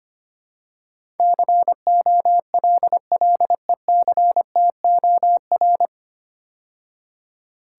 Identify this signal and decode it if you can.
Morse code, heard as a keyed tone, about 25 wpm, 700 Hz. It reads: COLLECTOR
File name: practice / 25wpm / 9 Letter Words - First 3 Letters Given - 25wpm